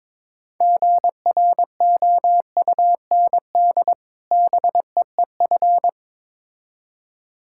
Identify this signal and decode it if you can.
Morse code, heard as a keyed tone, about 22 wpm, 690 Hz. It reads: GROUND BEEF
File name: Difficult Words with Hints - 22wpm